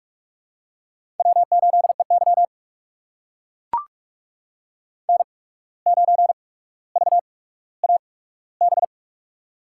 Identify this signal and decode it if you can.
Morse code, heard as a keyed tone, about 44 wpm, spaced as for 18 wpm, 700 Hz. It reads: W8EY N9VAB
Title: USA Call Signs - 8x Character Spacing - 45wpm